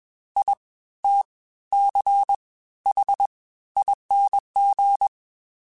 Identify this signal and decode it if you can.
Morse code, heard as a keyed tone, about 21 wpm, spaced as for 14 wpm, 800 Hz. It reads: ITCHING